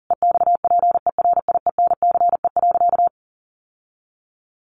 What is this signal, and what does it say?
Morse code, heard as a keyed tone, about 40 wpm, 700 Hz. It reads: EXPERIENCE.
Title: Sentences from Top 700 Words 40wpm